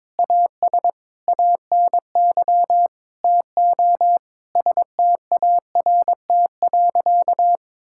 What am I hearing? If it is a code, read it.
Morse code, heard as a keyed tone, about 22 wpm, 700 Hz. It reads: AS ANY TO START.
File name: practice / 22wpm / Sentences with 8-Word Phrases - 22wpm